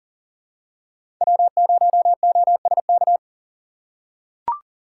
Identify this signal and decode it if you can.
Morse code, heard as a keyed tone, about 37 wpm, 700 Hz. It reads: W0OSK
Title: USA Call Signs - 4x Character Spacing - 40wpm